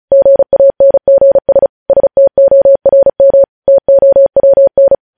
Morse code, heard as a keyed tone, about 36 wpm, 550 Hz. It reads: GANGS STORM TOWN
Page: news headlines in morse code at 35 wpm